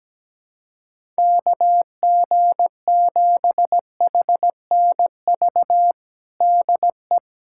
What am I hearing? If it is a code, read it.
Morse code, heard as a keyed tone, about 17 wpm, 700 Hz. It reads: KG7HNV DE
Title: USA Callsign DE Callsign - 6 Characters - 17wpm